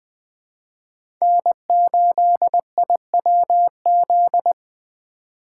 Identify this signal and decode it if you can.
Morse code, heard as a keyed tone, about 20 wpm, 700 Hz. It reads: N8IWZ
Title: Speed Racing - USA 1x3 Call Signs 17wpm